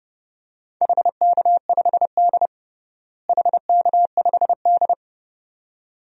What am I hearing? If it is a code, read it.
Morse code, heard as a keyed tone, about 32 wpm, 700 Hz. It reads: HK5D HK5D
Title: DXCC-100 Call Signs - 4 Characters - Repeated 2x - 30wpm